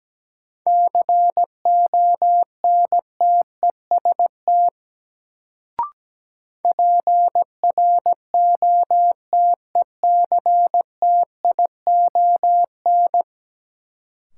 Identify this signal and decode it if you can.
Morse code, heard as a keyed tone, about 17 wpm, 700 Hz. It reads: CONTEST PROTECTION
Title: Sentences from Top 1600 Words - New Word Drill Alt - 17wpm